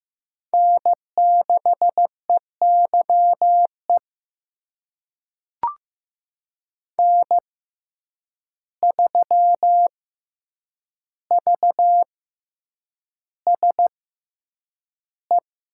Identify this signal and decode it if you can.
Morse code, heard as a keyed tone, about 15 wpm, spaced as for 8 wpm, 700 Hz. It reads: N6EYE N3VSE
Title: USA Call Signs - 5 Characters - 6x Character Spacing - 15wpm